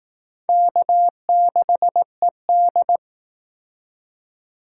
Morse code, heard as a keyed tone, about 18 wpm, 700 Hz. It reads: K6ED
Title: Speed Racing - USA 1x2 Call Signs 15wpm